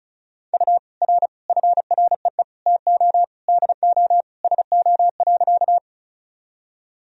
Morse code, heard as a keyed tone, about 36 wpm, 700 Hz. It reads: U R FREE TO DO SO.
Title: Sentences from Top 500 Words and Abbreviations 35wpm